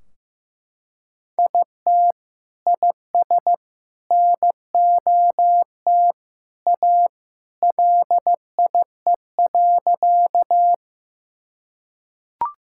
Morse code, heard as a keyed tone, about 15 wpm, 700 Hz. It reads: IT IS NOT A LIE.
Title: Sentences from Top 1600 Words 15wpm